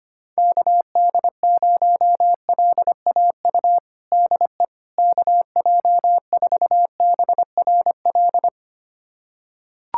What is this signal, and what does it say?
Morse code, heard as a keyed tone, about 25 wpm, 700 Hz. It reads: KD0LAU DE KJ4BRL
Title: USA Callsign DE Callsign 25wpm